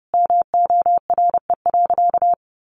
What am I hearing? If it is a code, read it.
Morse code, heard as a keyed tone, about 30 wpm, 700 Hz. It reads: MORE.
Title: Sentences from Top 1200 Words 30wpm